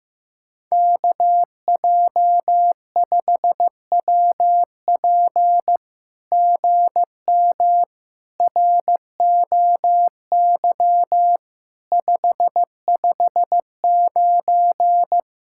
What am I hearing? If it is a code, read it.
Morse code, heard as a keyed tone, about 15 wpm, 700 Hz. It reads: KJ5WP GM ROY 559
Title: SOTA QSO Practice 15wpm